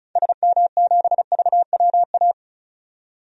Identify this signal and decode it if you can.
Morse code, heard as a keyed tone, about 36 wpm, 700 Hz. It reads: SM7VWA
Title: DXCC-100 Call Signs - 6 Characters - 35wpm